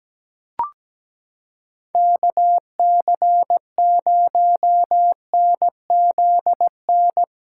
Morse code, heard as a keyed tone, about 17 wpm, 700 Hz. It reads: KC0NZN
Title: USA Callsign DE Callsign 17wpm